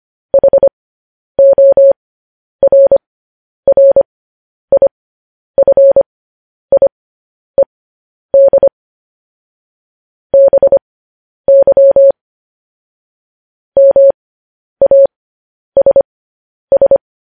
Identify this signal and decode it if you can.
Morse code, heard as a keyed tone, about 25 wpm, spaced as for 10 wpm, 550 Hz. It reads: HORRIFIED BY MASS
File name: podcast / News headlines-1x-25x10